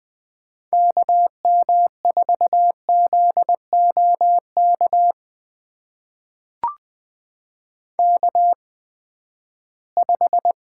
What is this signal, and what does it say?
Morse code, heard as a keyed tone, about 20 wpm, spaced as for 14 wpm, 700 Hz. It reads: KM4ZOK K5
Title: USA Call Signs - 8x Character Spacing - 20wpm